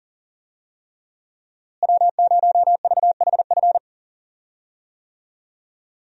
Morse code, heard as a keyed tone, about 40 wpm, 690 Hz. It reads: W0VHF
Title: USA Call Signs - 5 Characters - Repeated 3x - 40wpm